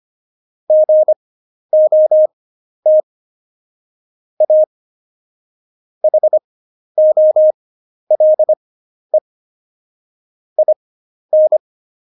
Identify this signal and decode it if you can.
Morse code, heard as a keyed tone, about 25 wpm, spaced as for 11 wpm, 620 Hz. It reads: GOT A HOLE IN
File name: Jokes-25-char-10-eff